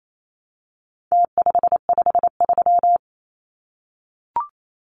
Morse code, heard as a keyed tone, about 28 wpm, 700 Hz. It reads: T553
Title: Numbers - Contest Style - 4-Digit 28wpm